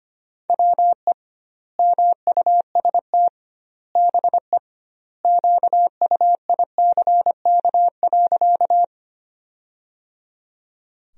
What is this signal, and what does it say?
Morse code, heard as a keyed tone, about 25 wpm, 700 Hz. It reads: WE MUST BE QUICK.
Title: 2x Word-Spacing - Sentences from Top 500 Words - 25wpm